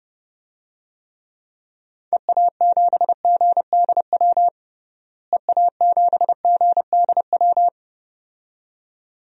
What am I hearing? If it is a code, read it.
Morse code, heard as a keyed tone, about 30 wpm, 700 Hz. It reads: EA7GDW EA7GDW
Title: DXCC-100 Call Signs - 6 Characters - Repeated 2x - 30wpm